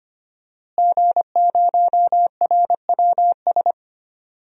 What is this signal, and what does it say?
Morse code, heard as a keyed tone, about 25 wpm, 700 Hz. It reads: G0RWS
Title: DXCC-100 Call Signs - 5 Characters - Repeated 2x - 25wpm